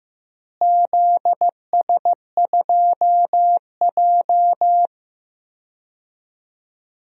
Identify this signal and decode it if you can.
Morse code, heard as a keyed tone, about 15 wpm, 700 Hz. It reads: ZS2J
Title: DXCC-100 Call Signs - 4 Characters - Repeated 3x - 15wpm